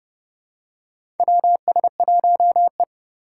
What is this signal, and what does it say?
Morse code, heard as a keyed tone, about 30 wpm, 700 Hz. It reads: WS1E
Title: USA Call Signs - 4x Character Spacing - 30wpm